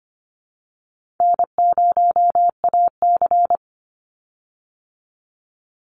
Morse code, heard as a keyed tone, about 25 wpm, 700 Hz. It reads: N0AC
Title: Easy Call Signs Repeated 3x - 25wpm